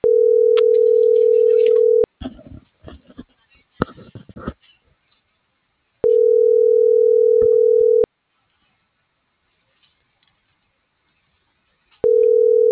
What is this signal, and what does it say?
Ringback tone (North America), 440 + 480 Hz.